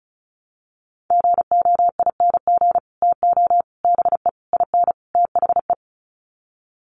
Morse code, heard as a keyed tone, about 34 wpm, 700 Hz. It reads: GOING TO BE IN THE
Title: Sets of 5 Words Encore 35wpm